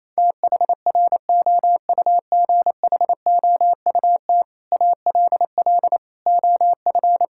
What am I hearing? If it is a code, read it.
Morse code, heard as a keyed tone, about 27 wpm, 700 Hz. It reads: THROUGHOUT ALL OF
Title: Sentences from Top 1000 Words - Review Part 5 - 28wpm